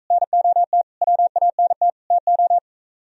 Morse code, heard as a keyed tone, about 44 wpm, 700 Hz. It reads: NOT WANT TO